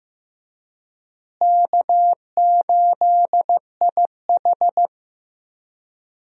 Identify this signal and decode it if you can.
Morse code, heard as a keyed tone, about 15 wpm, 700 Hz. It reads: K8IH
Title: USA Call Signs - 4 Characters - Repeated 2x - 15wpm